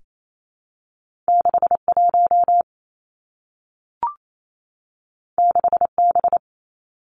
Morse code, heard as a keyed tone, about 28 wpm, 700 Hz. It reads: 6 1 6 B
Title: Two Challenging Letter-Number Combinations 28wpm